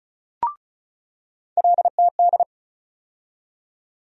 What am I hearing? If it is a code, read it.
Morse code, heard as a keyed tone, about 36 wpm, 700 Hz. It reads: LTD
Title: USA Call Sign Postfixes - 3 Letters 35wpm